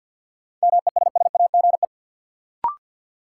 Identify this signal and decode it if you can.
Morse code, heard as a keyed tone, about 53 wpm, 700 Hz. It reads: MESSAGE
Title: Sentences from Top 1100 Words - New Word Drill Alt - 50wpm